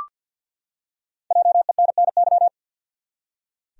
Morse code, heard as a keyed tone, about 52 wpm, 700 Hz. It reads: JENNY
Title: USA Common Names - Max 5 Letters - 50wpm